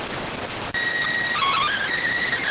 SSTV (slow-scan television), PD 290.